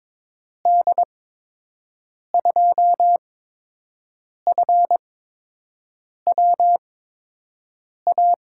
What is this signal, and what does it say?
Morse code, heard as a keyed tone, about 22 wpm, spaced as for 6 wpm, 700 Hz. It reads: D2FWA
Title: USA Call Signs - 6 Characters - 8x Character Spacing - 22wpm